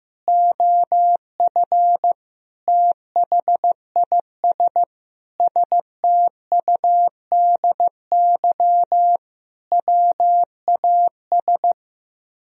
Morse code, heard as a keyed tone, about 15 wpm, 700 Hz. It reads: OF THIS STUDY WAS
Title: Speed-Racing - Sets of 5 Words 15wpm